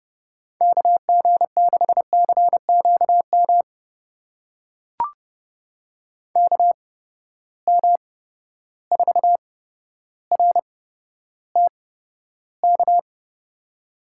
Morse code, heard as a keyed tone, about 31 wpm, spaced as for 13 wpm, 700 Hz. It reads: KG6CQM KM4RTK
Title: USA Call Signs - 6 Characters - 8x Character Spacing - 30wpm